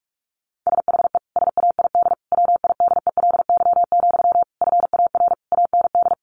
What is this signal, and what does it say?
Morse code, heard as a keyed tone, about 44 wpm, 700 Hz. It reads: SHE SAID WIDELY, FAR AND
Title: Sentences from Top 1200 Words 45wpm